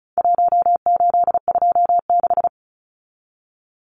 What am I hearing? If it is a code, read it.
Morse code, heard as a keyed tone, about 34 wpm, 700 Hz. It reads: 1826
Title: Numbers - Contest Style - 4-Digit 35wpm